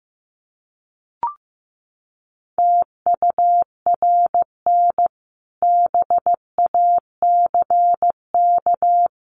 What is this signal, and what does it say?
Morse code, heard as a keyed tone, about 15 wpm, 700 Hz. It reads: TURN BACK
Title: Sentences from Top 1300 Words 15wpm